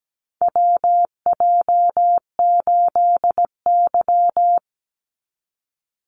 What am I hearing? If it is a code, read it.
Morse code, heard as a keyed tone, about 17 wpm, 700 Hz. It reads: WJ8Y
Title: Call Signs Repeated 2x - 17wpm